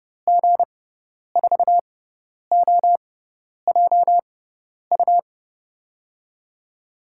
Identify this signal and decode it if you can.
Morse code, heard as a keyed tone, about 32 wpm, spaced as for 10 wpm, 700 Hz. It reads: G4OJU
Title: USA Call Signs - 6 Characters - 6x Character Spacing - 30wpm